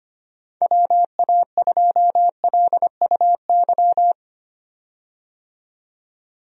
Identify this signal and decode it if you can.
Morse code, heard as a keyed tone, about 25 wpm, 700 Hz. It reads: WA2LUY
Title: USA Call Signs - 6 Characters - 25wpm